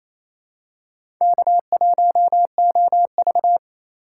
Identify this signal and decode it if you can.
Morse code, heard as a keyed tone, about 28 wpm, 700 Hz. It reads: K1OV
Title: USA Call Signs - 4 Characters - 2x Character Spacing - 28wpm